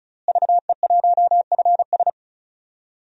Morse code, heard as a keyed tone, about 34 wpm, 700 Hz. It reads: VE1FS
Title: DXCC-100 Call Signs - 5 Characters - Repeated 2x - 35wpm